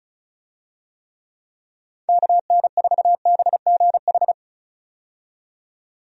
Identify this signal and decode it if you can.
Morse code, heard as a keyed tone, about 36 wpm, 700 Hz. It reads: KN4BGH
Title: USA Call Signs - 6 Characters - 35wpm